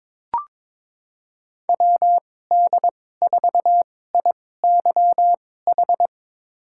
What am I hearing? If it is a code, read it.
Morse code, heard as a keyed tone, about 22 wpm, spaced as for 16 wpm, 700 Hz. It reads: WD4IYH
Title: USA Call Signs - 6 Characters - 2x Character Spacing - 22wpm